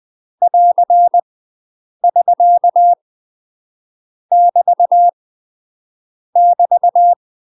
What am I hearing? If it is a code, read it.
Morse code, heard as a keyed tone, about 20 wpm, 700 Hz. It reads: <AR> <SK> <BT> <BT>
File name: UK MASTER QSOs TRAINING / 20WPM / UK MASTER QSOs 0032